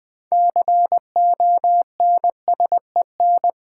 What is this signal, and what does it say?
Morse code, heard as a keyed tone, about 20 wpm, 700 Hz. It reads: CONSEN
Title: Build A Word - 3 Letters At A Time - 9-Letter Words - 20wpm